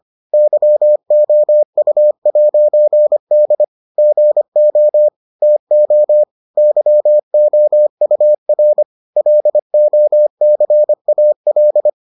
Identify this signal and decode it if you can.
Morse code, heard as a keyed tone, about 25 wpm, 600 Hz. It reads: YOU'D GO TO YOUR LOCAL